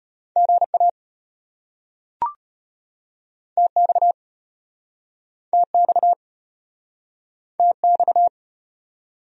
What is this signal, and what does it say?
Morse code, heard as a keyed tone, about 33 wpm, 700 Hz. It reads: GA TX TX TX